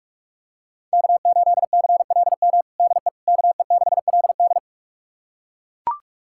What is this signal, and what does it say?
Morse code, heard as a keyed tone, about 41 wpm, 700 Hz. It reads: K8CLM DE KE6LD
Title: USA Callsign DE Callsign - 5 Characters - 45wpm